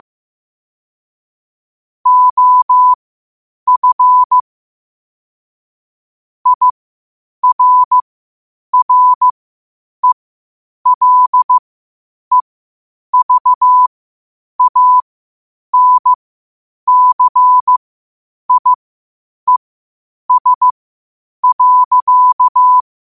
Morse code, heard as a keyed tone, about 15 wpm, spaced as for 8 wpm, 1000 Hz. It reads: OF IRRELEVANCIES.